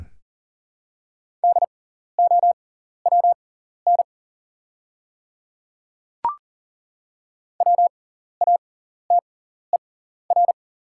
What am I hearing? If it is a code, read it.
Morse code, heard as a keyed tone, about 40 wpm, spaced as for 14 wpm, 700 Hz. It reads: DOWN WATER
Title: 6x Character Spacing - Top 100 Words - 40wpm